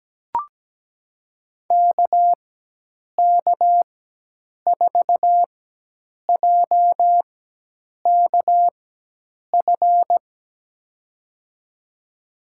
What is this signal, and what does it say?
Morse code, heard as a keyed tone, about 17 wpm, spaced as for 8 wpm, 700 Hz. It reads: KK4JKF
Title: USA Call Signs - 6 Characters - 4x Character Spacing - 17wpm